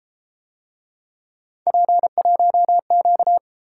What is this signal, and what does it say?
Morse code, heard as a keyed tone, about 31 wpm, 700 Hz. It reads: P1Q